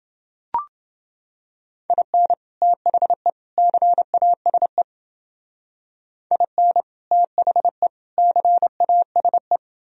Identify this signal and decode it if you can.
Morse code, heard as a keyed tone, about 29 wpm, 700 Hz. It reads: IN THE CASE IN THE CASE